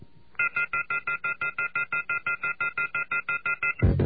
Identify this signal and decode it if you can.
Off-hook warning tone, 1400 + 2060 + 2450 + 2600 Hz, with noise.
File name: off the hook